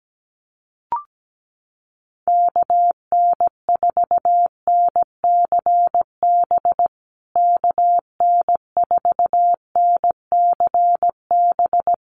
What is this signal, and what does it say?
Morse code, heard as a keyed tone, about 17 wpm, 700 Hz. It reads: KN4NCB KN4NCB
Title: USA Call Signs Repeated 3x 17wpm